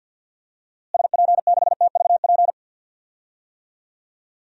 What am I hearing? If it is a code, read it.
Morse code, heard as a keyed tone, about 49 wpm, 700 Hz. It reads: SP6TVP
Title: DXCC-100 Call Signs - 6 Characters - Repeated 2x - 50wpm